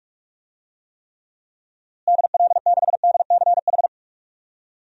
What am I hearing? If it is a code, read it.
Morse code, heard as a keyed tone, about 41 wpm, 700 Hz. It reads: DL6DCH
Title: DXCC-100 Call Signs - 6 Characters - Repeated 3x - 45wpm